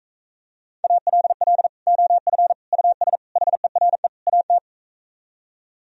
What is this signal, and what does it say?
Morse code, heard as a keyed tone, about 45 wpm, 700 Hz. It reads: ALL OF US HERE AT